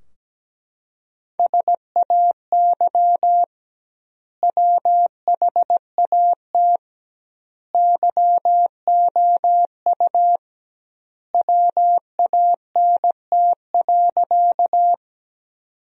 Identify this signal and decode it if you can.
Morse code, heard as a keyed tone, about 17 wpm, 700 Hz. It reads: SAY WHAT YOU WANT.
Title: 2x Word-Spacing - Sentences from Top 200 Words - 17wpm